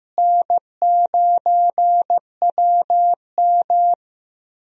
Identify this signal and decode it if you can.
Morse code, heard as a keyed tone, about 15 wpm, 700 Hz. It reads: N9WM
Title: USA Call Signs - 4 Characters - Repeated 2x - 15wpm